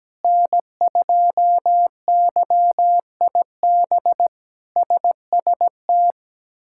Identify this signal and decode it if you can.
Morse code, heard as a keyed tone, about 17 wpm, 700 Hz. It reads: N2YIB SST
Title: K1USN SST QSO Practice - 17wpm